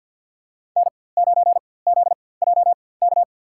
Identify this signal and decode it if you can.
Morse code, heard as a keyed tone, about 53 wpm, spaced as for 24 wpm, 700 Hz. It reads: N9ZJK
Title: USA Call Signs - 4x Character Spacing - 50wpm